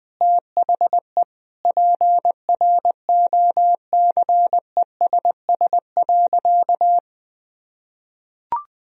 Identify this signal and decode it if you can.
Morse code, heard as a keyed tone, about 20 wpm, 700 Hz. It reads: THE PROCESS.
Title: Sentences from Top 1000 Words - Review Part 3 - 20wpm